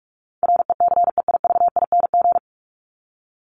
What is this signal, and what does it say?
Morse code, heard as a keyed tone, about 44 wpm, 700 Hz. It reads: RECEIVING
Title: ING Suffix - Top 100 Words with Base Words 45wpm